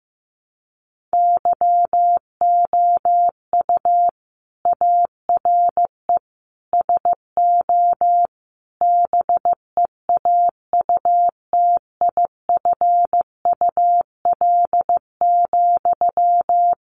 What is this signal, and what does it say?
Morse code, heard as a keyed tone, about 15 wpm, 700 Hz. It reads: YOU ARE SO BEAUTIFUL,
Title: Sentences from Top 1200 Words 15wpm